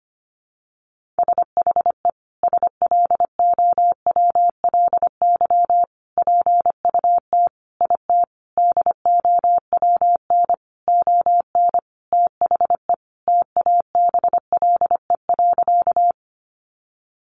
Morse code, heard as a keyed tone, about 25 wpm, 700 Hz. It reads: SHE SLOWLY PUT IT DOWN ON THE TABLE.
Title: Sentences from Top 1100 Words 25wpm